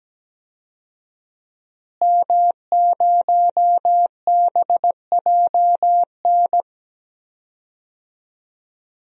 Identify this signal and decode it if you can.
Morse code, heard as a keyed tone, about 17 wpm, 700 Hz. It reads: M0BJN
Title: DXCC-100 Call Signs - 5 Characters - Repeated 3x - 17wpm